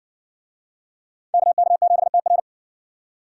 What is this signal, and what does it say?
Morse code, heard as a keyed tone, about 58 wpm, 700 Hz. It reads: KK7TR